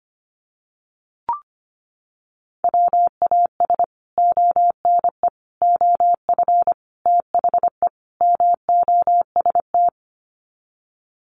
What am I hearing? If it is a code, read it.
Morse code, heard as a keyed tone, about 25 wpm, 700 Hz. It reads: WAS ONE OF THE MOST
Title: Sets of 5 Words 25wpm